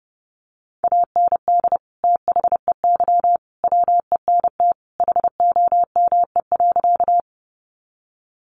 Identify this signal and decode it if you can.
Morse code, heard as a keyed tone, about 30 wpm, 700 Hz. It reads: AND THEY WENT HOME.
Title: Sentences from Top 200 Words 30wpm